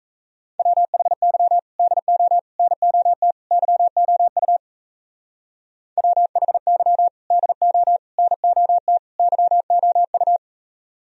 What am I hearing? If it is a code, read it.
Morse code, heard as a keyed tone, about 38 wpm, 700 Hz. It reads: WHY DO NOT YOU WHY DO NOT YOU